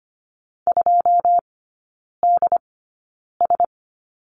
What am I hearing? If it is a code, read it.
Morse code, heard as a keyed tone, about 25 wpm, spaced as for 9 wpm, 690 Hz. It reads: 2DS
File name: Lesson 14e - Number 2 - Three Characters 25wpm